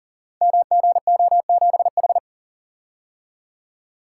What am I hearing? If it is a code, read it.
Morse code, heard as a keyed tone, about 39 wpm, 700 Hz. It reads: MGO7H
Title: Code Groups - 5 Letter:Numbers - 40wpm